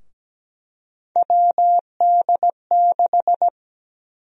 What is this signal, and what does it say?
Morse code, heard as a keyed tone, about 17 wpm, 700 Hz. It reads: WD6
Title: USA Call Sign Prefixes - 17wpm